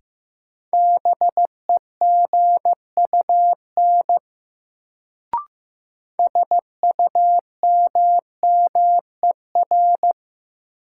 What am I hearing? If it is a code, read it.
Morse code, heard as a keyed tone, about 15 wpm, 700 Hz. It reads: BEGUN SUMMER
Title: Sentences from Top 1000 Words - Review Part 6 - New Word Drill - 15wpm